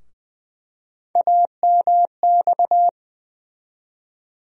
Morse code, heard as a keyed tone, about 20 wpm, 700 Hz. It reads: AMX
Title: USA Call Sign Postfixes - 3 Letters 20wpm